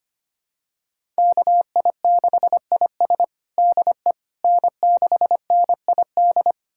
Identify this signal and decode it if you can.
Morse code, heard as a keyed tone, about 25 wpm, 700 Hz. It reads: KI6IS DE N6NID
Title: USA Callsign DE Callsign - 5 Characters - 25wpm